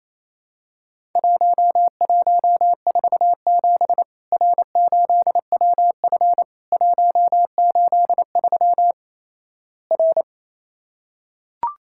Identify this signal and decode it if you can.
Morse code, heard as a keyed tone, about 27 wpm, 700 Hz. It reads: 1147 R8WF 183 R
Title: Worked All Europe DX Contest - QTC Exchange - 28wpm